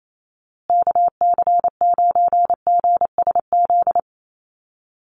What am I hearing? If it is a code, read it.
Morse code, heard as a keyed tone, about 28 wpm, 700 Hz. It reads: KC9GSZ
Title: USA Call Signs Repeated 3x 28wpm